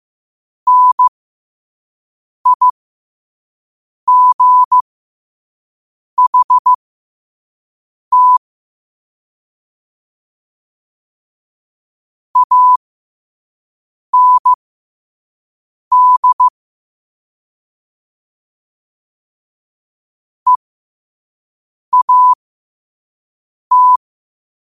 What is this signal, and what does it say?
Morse code, heard as a keyed tone, about 15 wpm, spaced as for 5 wpm, 1000 Hz. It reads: NIGHT AND EAT
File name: qotdfiles / 5wpm